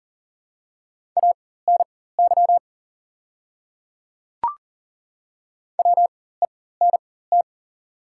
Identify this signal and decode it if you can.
Morse code, heard as a keyed tone, about 38 wpm, spaced as for 18 wpm, 700 Hz. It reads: ANY WENT
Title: 4x Character Spacing - Top 300 Words - 40wpm